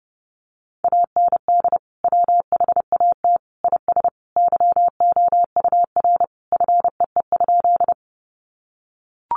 Morse code, heard as a keyed tone, about 30 wpm, 700 Hz. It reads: AND WHAT IS YOUR FEE?
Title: Sentences from Top 1500 Words 30wpm